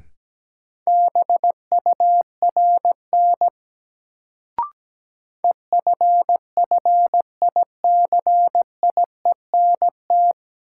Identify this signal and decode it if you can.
Morse code, heard as a keyed tone, about 17 wpm, 700 Hz. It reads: BURN EFFICIENT
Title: Sentences from Top 1300 Words - New Word Drill - 17wpm